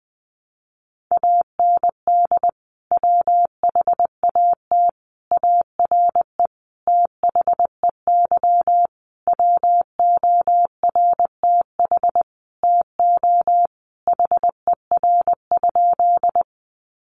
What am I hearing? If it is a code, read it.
Morse code, heard as a keyed tone, about 20 wpm, 700 Hz. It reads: AND WHAT ARE THEY WORTH TO HER?
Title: Sentences from Top 800 Words 20wpm